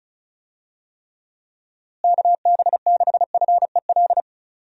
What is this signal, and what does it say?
Morse code, heard as a keyed tone, about 34 wpm, 700 Hz. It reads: KB6FEL
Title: USA Call Signs - 6 Characters - Repeated 2x - 35wpm